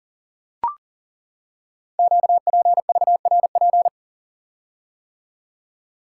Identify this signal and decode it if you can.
Morse code, heard as a keyed tone, about 40 wpm, 700 Hz. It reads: QPVRP
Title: Code Groups - 5 Letters - 40wpm